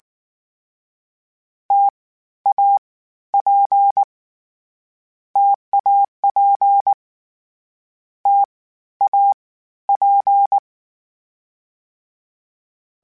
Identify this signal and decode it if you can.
Morse code, heard as a keyed tone, about 19 wpm, 800 Hz. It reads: T A P TAP T A P
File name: Difficult Combos / Difficult Combos-mp3s / T AOEI @18